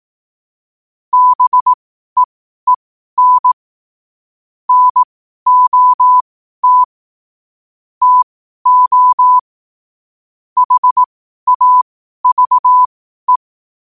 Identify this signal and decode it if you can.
Morse code, heard as a keyed tone, about 18 wpm, spaced as for 12 wpm, 1000 Hz. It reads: BEEN NOT TO HAVE
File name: qotdfiles / 12wpm